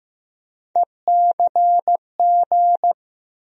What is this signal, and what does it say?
Morse code, heard as a keyed tone, about 15 wpm, 700 Hz. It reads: ECG
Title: USA Call Sign Postfixes - 3 Letters 15wpm